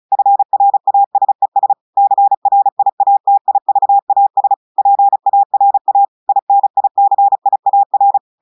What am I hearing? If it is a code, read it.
Morse code, heard as a keyed tone, about 36 wpm, 800 Hz. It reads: FRASES CRIATIVAS PARA INICIAR
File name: portugais texte en clair 2 vitesse 35 mots minutes